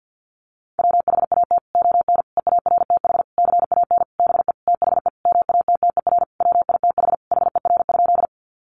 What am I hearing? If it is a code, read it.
Morse code, heard as a keyed tone, about 50 wpm, 700 Hz. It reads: WHAT ON EARTH CAN BE THE MATTER WITH HER?
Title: Sentences from Top 700 Words 50wpm